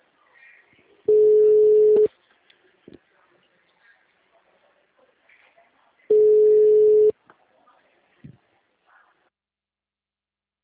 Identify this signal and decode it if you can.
Ringback tone (Europe, 425 Hz).
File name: out- -2019- - - .21406